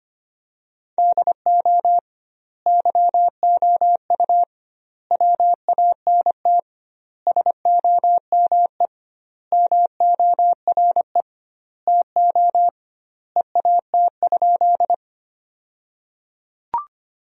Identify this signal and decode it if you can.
Morse code, heard as a keyed tone, about 25 wpm, 700 Hz. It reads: DO YOU WANT SOME MORE TO EAT?
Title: 2x Word-Spacing - Sentences from Top 300 Words - 25wpm